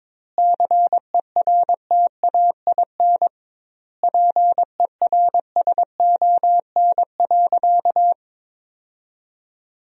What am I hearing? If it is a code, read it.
Morse code, heard as a keyed tone, about 22 wpm, 700 Hz. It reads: CERTAIN PERSON.
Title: 2x Word-Spacing - Sentences from Top 500 Words - 22wpm